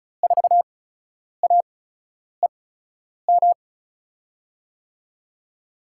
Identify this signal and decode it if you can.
Morse code, heard as a keyed tone, about 36 wpm, spaced as for 10 wpm, 700 Hz. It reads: 4AEM
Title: USA Call Signs - 6 Characters - 8x Character Spacing - 35wpm